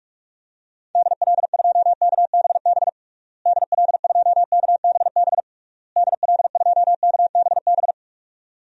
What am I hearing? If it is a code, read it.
Morse code, heard as a keyed tone, about 41 wpm, 700 Hz. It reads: DL2KBB DL2KBB DL2KBB
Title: DXCC-100 Call Signs - 6 Characters - Repeated 3x - 45wpm